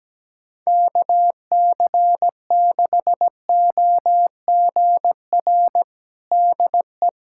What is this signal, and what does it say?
Morse code, heard as a keyed tone, about 17 wpm, 700 Hz. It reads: KC6OGR DE
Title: USA Callsign DE Callsign 17wpm